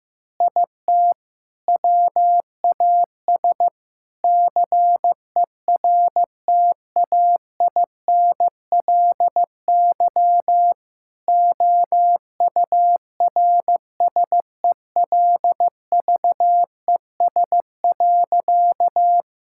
Morse code, heard as a keyed tone, about 15 wpm, 700 Hz. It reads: IT WAS CERTAINLY OURSELVES.
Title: Sentences from Top 1600 Words 15wpm